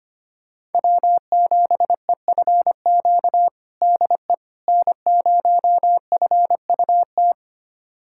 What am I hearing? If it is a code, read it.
Morse code, heard as a keyed tone, about 25 wpm, 700 Hz. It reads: W7EFQ DE N0FUT
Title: USA Callsign DE Callsign - 5 Characters - 25wpm